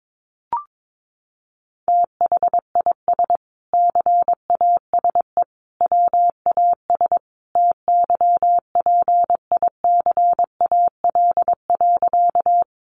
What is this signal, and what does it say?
Morse code, heard as a keyed tone, about 22 wpm, 700 Hz. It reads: THIS CASE WAS TYPICAL.
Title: Sentences from Top 1100 Words 22wpm